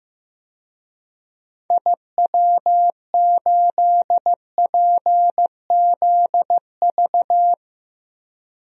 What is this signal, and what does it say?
Morse code, heard as a keyed tone, about 15 wpm, 700 Hz. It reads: IW8PZV
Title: DXCC-100 Call Signs - 6 Characters - Repeated 3x - 15wpm